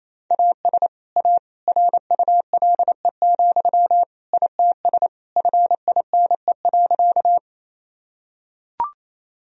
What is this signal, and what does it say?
Morse code, heard as a keyed tone, about 28 wpm, 700 Hz. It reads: AS A RULE, ITS FINE.
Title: Sentences from Top 1000 Words - Review Part 2 - 28wpm